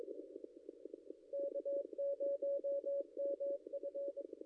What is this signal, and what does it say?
Morse code, heard as a keyed tone, about 22 wpm, 560 Hz. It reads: K0MF
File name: contests / ss-cw-k5tr-op-n5rz -0800